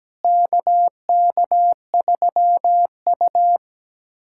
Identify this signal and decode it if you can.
Morse code, heard as a keyed tone, about 17 wpm, 700 Hz. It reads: KK3U
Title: USA Callsign DE Callsign - 4 Characters - 17wpm